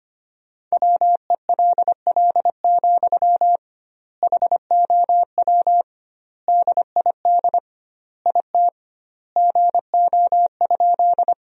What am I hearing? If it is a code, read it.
Morse code, heard as a keyed tone, about 25 wpm, 700 Hz. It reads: WELL, HOW DID IT GO?
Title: 2x Word-Spacing - Sentences from Top 200 Words - 25wpm